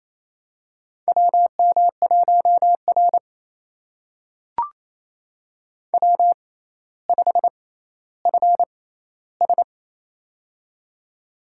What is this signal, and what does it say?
Morse code, heard as a keyed tone, about 27 wpm, spaced as for 14 wpm, 700 Hz. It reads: WM1R W5FS
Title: USA Call Signs - 4 Characters - 6x Character Spacing - 28wpm